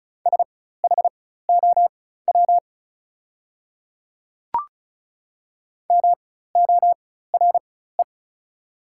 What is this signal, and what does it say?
Morse code, heard as a keyed tone, about 36 wpm, spaced as for 16 wpm, 700 Hz. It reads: SHOW MORE
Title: 4x Character Spacing - Top 200 Words - 35wpm